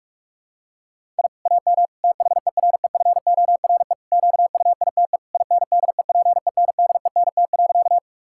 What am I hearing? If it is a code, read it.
Morse code, heard as a keyed tone, about 46 wpm, 700 Hz. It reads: I AM THEREFORE QUITE INDEPENDENT.
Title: Sentences from Top 1000 Words - Review Part 6 - 45wpm